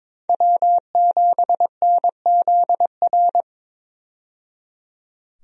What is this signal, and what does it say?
Morse code, heard as a keyed tone, about 22 wpm, 700 Hz. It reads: W7NZR
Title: USA Call Signs - 5 Characters - 22wpm